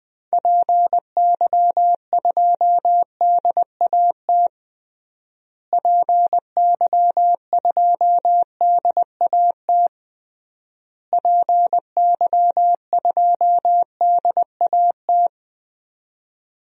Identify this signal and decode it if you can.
Morse code, heard as a keyed tone, about 21 wpm, 700 Hz. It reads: PY2DAT PY2DAT PY2DAT
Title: DXCC-100 Call Signs - 6 Characters - Repeated 3x - 20wpm